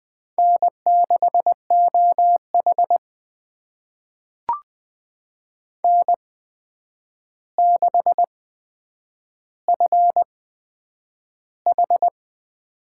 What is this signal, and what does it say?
Morse code, heard as a keyed tone, about 20 wpm, spaced as for 9 wpm, 700 Hz. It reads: N6OH N6FH
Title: USA Call Signs - 8x Character Spacing - 20wpm